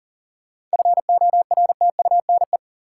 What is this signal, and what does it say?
Morse code, heard as a keyed tone, about 40 wpm, 700 Hz. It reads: FORTUNE
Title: Seven Letter Words - 40wpm